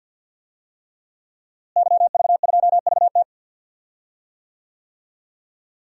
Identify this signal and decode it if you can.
Morse code, heard as a keyed tone, about 46 wpm, 700 Hz. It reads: YV2VT
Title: DXCC-100 Call Signs - 5 Characters - Repeated 3x - 50wpm